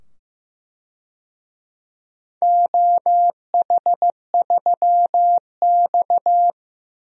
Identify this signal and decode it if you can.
Morse code, heard as a keyed tone, about 15 wpm, 700 Hz. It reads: OH3X
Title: DXCC-100 Call Signs - 4 Characters - 15wpm